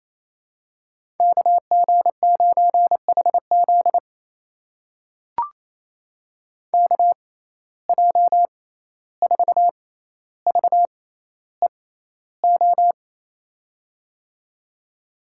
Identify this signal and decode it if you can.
Morse code, heard as a keyed tone, about 28 wpm, spaced as for 14 wpm, 700 Hz. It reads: KG9HZ KJ4VEO
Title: USA Call Signs - 6x Character Spacing - 28wpm